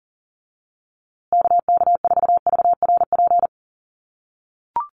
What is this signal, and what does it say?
Morse code, heard as a keyed tone, about 40 wpm, 700 Hz. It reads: KK4VRP
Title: USA Call Signs Repeated 3x 40wpm